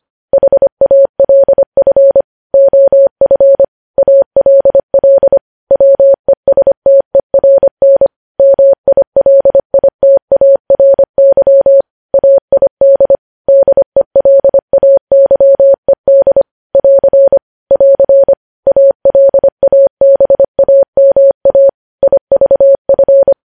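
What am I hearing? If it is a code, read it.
Morse code, heard as a keyed tone, about 25 wpm, 550 Hz. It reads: HALF OF ALL WESTERN MILITARY AID DELAYED <AR> <AR> ALABAMA IVF
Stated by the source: Morse code news headlines sent at 25 wpm.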